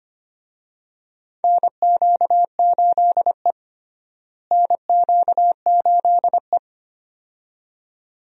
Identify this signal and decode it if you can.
Morse code, heard as a keyed tone, about 25 wpm, 700 Hz. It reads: NQ8E NQ8E
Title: USA Call Signs - 4 Characters - Repeated 2x - 25wpm